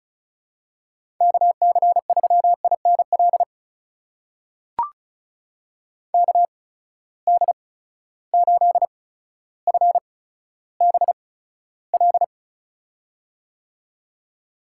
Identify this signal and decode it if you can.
Morse code, heard as a keyed tone, about 36 wpm, spaced as for 15 wpm, 700 Hz. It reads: KC3INL KD8FBL
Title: USA Call Signs - 6 Characters - 8x Character Spacing - 35wpm